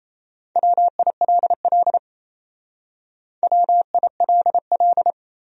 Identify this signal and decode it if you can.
Morse code, heard as a keyed tone, about 35 wpm, 700 Hz. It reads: WILL WILL
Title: Speed-Racing Top 300 Words 28wpm